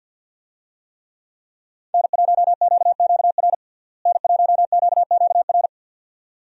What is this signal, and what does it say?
Morse code, heard as a keyed tone, about 51 wpm, 700 Hz. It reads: N1QQR N1QQR
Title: USA Call Signs - 5 Characters - Repeated 2x - 50wpm